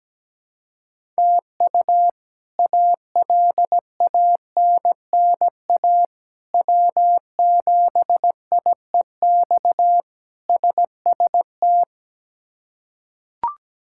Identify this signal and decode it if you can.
Morse code, heard as a keyed tone, about 17 wpm, 700 Hz. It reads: TU ALANNA W7IEX SST
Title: K1USN SST QSO Practice - 17wpm